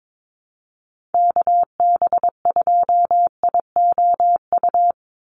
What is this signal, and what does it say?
Morse code, heard as a keyed tone, about 22 wpm, 700 Hz. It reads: KB2IOU
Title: USA Call Signs 22wpm